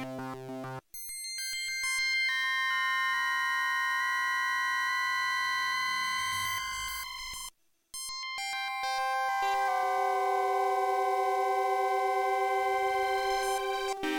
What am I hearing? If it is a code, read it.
Dial-up modem handshake.